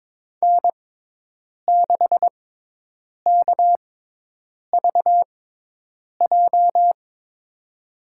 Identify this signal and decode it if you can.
Morse code, heard as a keyed tone, about 22 wpm, spaced as for 8 wpm, 700 Hz. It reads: N6KVJ
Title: USA Call Signs - 5 Characters - 6x Character Spacing - 22wpm